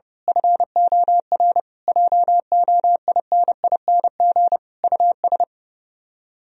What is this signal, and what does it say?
Morse code, heard as a keyed tone, about 28 wpm, 700 Hz. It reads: FOR JOINING US
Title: Speed-Racing - Sets of 5 Words 30wpm